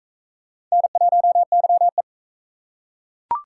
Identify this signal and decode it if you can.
Morse code, heard as a keyed tone, about 40 wpm, 700 Hz. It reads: N1YE